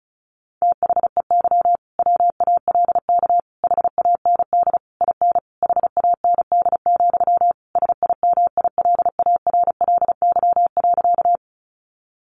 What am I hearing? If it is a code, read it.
Morse code, heard as a keyed tone, about 34 wpm, 700 Hz. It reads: THEY WALK HAND IN HAND, SIMILARLY.
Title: Sentences from Top 1400 Words 35wpm